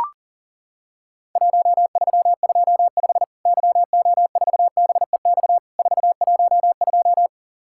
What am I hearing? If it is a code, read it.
Morse code, heard as a keyed tone, about 40 wpm, 700 Hz. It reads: 1325 YO4BEX 412
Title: Worked All Europe DX Contest - QTC Exchange - 40wpm